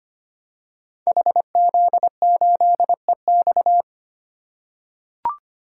Morse code, heard as a keyed tone, about 25 wpm, 700 Hz. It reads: HZ8EX
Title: Code Groups - 5 Letter:Numbers - 25wpm